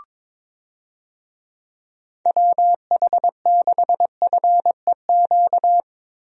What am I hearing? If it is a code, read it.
Morse code, heard as a keyed tone, about 23 wpm, 700 Hz. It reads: WH6FEQ
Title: USA Call Signs - 6 Characters - Repeated 2x - 22wpm